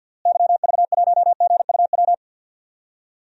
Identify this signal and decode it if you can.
Morse code, heard as a keyed tone, about 51 wpm, 700 Hz. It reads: YV1GUW
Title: DXCC-100 Call Signs - 6 Characters - Repeated 3x - 50wpm